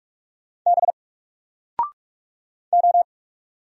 Morse code, heard as a keyed tone, about 44 wpm, 700 Hz. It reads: B O
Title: Single Letter-Number 45wpm